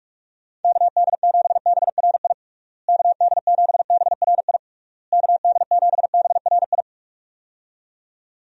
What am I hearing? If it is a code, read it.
Morse code, heard as a keyed tone, about 46 wpm, 700 Hz. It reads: KD7BRI KD7BRI KD7BRI
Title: USA Call Signs - 6 Characters - Repeated 3x - 45wpm